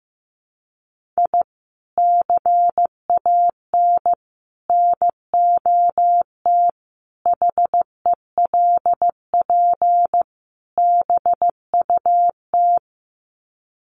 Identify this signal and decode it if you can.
Morse code, heard as a keyed tone, about 15 wpm, 700 Hz. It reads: I CAN NOT HELP BUT
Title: Sets of 5 Words Encore2 15wpm